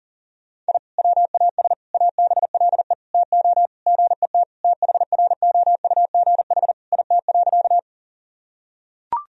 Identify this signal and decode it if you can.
Morse code, heard as a keyed tone, about 40 wpm, 700 Hz. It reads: I WAS ABLE TO GET THROUGH IT.
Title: Sentences from Top 1000 Words - Review Part 2 - 40wpm